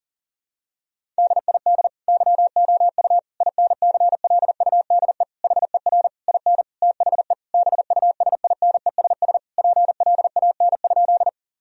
Morse code, heard as a keyed tone, about 40 wpm, 700 Hz. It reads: DID YOU INCLUDE HER IN THE BUSINESS PLAN?
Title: Sentences from Top 1000 Words - Review Part 3 - 40wpm